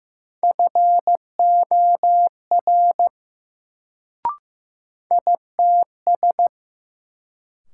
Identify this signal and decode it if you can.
Morse code, heard as a keyed tone, about 15 wpm, 700 Hz. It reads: FOR ITS
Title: Sets of 2 Words - New Word Drill - 15wpm